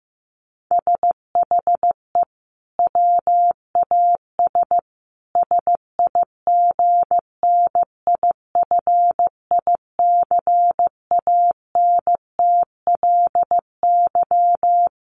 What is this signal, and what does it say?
Morse code, heard as a keyed tone, about 15 wpm, 700 Hz. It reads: SHE WAS SIGNIFICANTLY
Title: Sentences from Top 1500 Words 15wpm